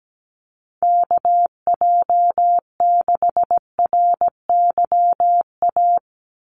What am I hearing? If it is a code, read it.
Morse code, heard as a keyed tone, about 17 wpm, 700 Hz. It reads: KJ6RYA
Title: USA Call Signs 17wpm